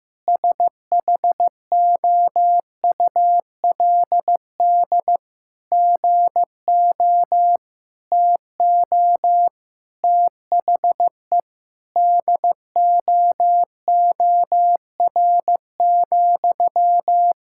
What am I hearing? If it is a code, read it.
Morse code, heard as a keyed tone, about 15 wpm, 700 Hz. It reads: SHOULD GO TO THE DOOR,
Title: Sentences from Top 1600 Words 15wpm